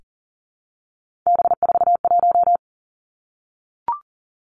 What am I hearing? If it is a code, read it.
Morse code, heard as a keyed tone, about 42 wpm, 690 Hz. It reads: B41
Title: Three Challenging Letter-Number Combinations 40wpm